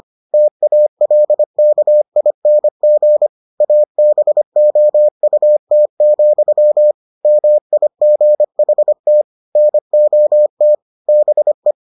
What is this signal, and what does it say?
Morse code, heard as a keyed tone, about 25 wpm, 600 Hz. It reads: TALKING ABOUT, MIGHT NOT BE